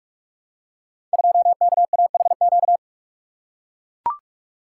Morse code, heard as a keyed tone, about 44 wpm, 700 Hz. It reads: 2KAHQ
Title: Code Groups - 5 Letter:Numbers - 45wpm